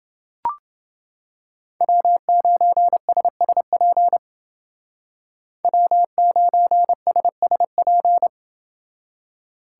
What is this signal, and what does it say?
Morse code, heard as a keyed tone, about 29 wpm, 700 Hz. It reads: W9SSP W9SSP